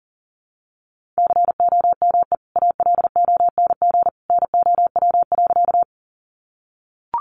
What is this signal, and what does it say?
Morse code, heard as a keyed tone, about 40 wpm, 700 Hz. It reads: COME ALONG NOW.
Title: Sentences from Top 700 Words 40wpm